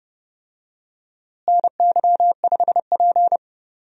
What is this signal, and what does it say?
Morse code, heard as a keyed tone, about 30 wpm, 700 Hz. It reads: NY5P
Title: USA Call Signs - 4 Characters - Repeated 2x - 30wpm